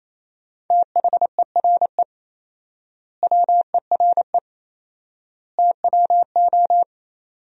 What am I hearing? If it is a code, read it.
Morse code, heard as a keyed tone, about 28 wpm, 700 Hz. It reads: THERE WERE TWO
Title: 4x Word-Spacing - Sentences from Top 100 Words - 28wpm